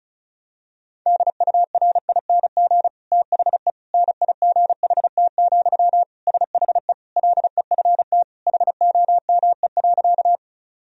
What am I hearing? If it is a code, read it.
Morse code, heard as a keyed tone, about 36 wpm, 700 Hz. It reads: DURING THE NIGHT, SHE LEFT HOME.
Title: Sentences from Top 1000 Words - Review Part 2 - 35wpm